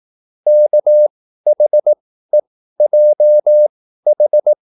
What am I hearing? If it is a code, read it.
Morse code, heard as a keyed tone, about 18 wpm, spaced as for 13 wpm, 600 Hz. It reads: KHEJH